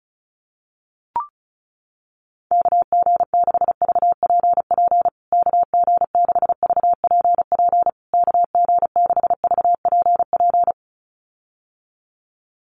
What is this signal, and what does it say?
Morse code, heard as a keyed tone, about 34 wpm, 700 Hz. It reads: KG6VPP KG6VPP KG6VPP
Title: USA Call Signs Repeated 3x 35wpm